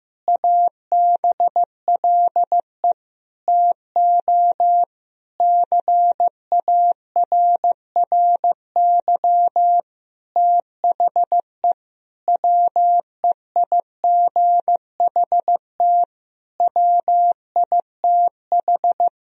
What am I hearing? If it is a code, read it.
Morse code, heard as a keyed tone, about 15 wpm, 700 Hz. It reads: ABLE TO CARRY THE WEIGHT WITH
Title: Sentences from Top 1000 Words - Review Part 3 - 15wpm